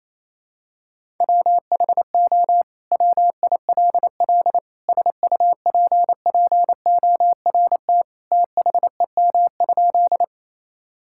Morse code, heard as a keyed tone, about 27 wpm, 700 Hz. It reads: WHO WILL SUPPORT THEM?
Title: Sentences from Top 1000 Words - Review Part 5 - 28wpm